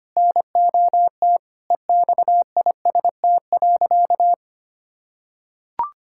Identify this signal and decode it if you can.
Morse code, heard as a keyed tone, about 25 wpm, 700 Hz. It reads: NOT EXIST.
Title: Sentences from Top 1000 Words - Review Part 5 - 25wpm